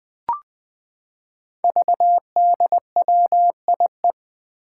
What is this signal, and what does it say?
Morse code, heard as a keyed tone, about 20 wpm, 700 Hz. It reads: VDWIE
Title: Five Challenging Letter-Number Combinations 20wpm